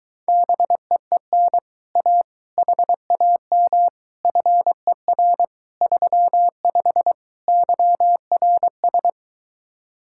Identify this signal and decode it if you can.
Morse code, heard as a keyed tone, about 23 wpm, 700 Hz. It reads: BEEN A HAM FER 35 YRS